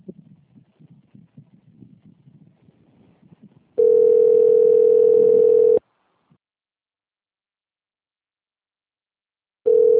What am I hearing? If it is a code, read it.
Ringback tone (North America), 440 + 480 Hz.
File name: rg-1600-unknown- - - .39431